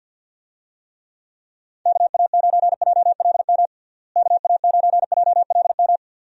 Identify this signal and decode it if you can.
Morse code, heard as a keyed tone, about 47 wpm, 700 Hz. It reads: KA9JLM KA9JLM
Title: USA Call Signs - 6 Characters - Repeated 2x - 50wpm